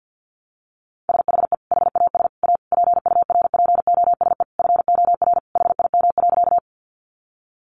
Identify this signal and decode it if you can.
Morse code, heard as a keyed tone, about 50 wpm, 700 Hz. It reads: SHE HAS A PURPOSE FOR HIM.
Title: Sentences from Top 800 Words 50wpm